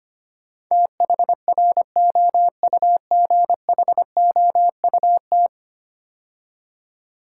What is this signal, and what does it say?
Morse code, heard as a keyed tone, about 25 wpm, 700 Hz. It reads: THROUGHOUT
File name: 10 Letter Words - First 3 Letters Given - 25wpm